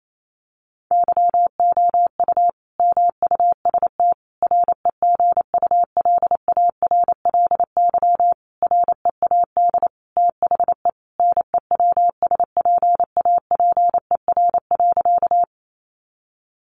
Morse code, heard as a keyed tone, about 28 wpm, 700 Hz. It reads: YOU MUST REGULARLY READ THE NEWSPAPER.
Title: Sentences from Top 1400 Words 28wpm